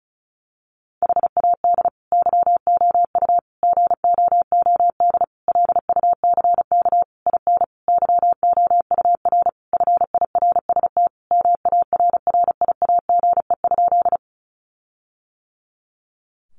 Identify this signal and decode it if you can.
Morse code, heard as a keyed tone, about 34 wpm, 700 Hz. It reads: HAD YOU GOOD LUCK IN YOUR FIRST MARRIAGE?
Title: Sentences from Top 1500 Words 35wpm